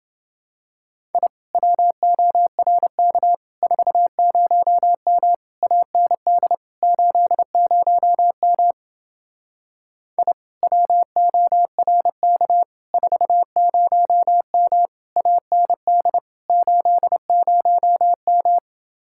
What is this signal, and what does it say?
Morse code, heard as a keyed tone, about 29 wpm, 700 Hz. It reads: I WORK 40M AND 80M I WORK 40M AND 80M